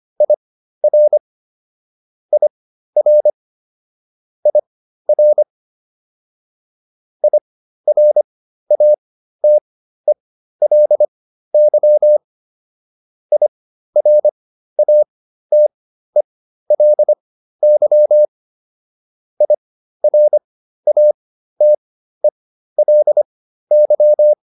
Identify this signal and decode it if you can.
Morse code, heard as a keyed tone, about 25 wpm, spaced as for 13 wpm, 600 Hz. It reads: IR IR IR IRATELY IRATELY IRATELY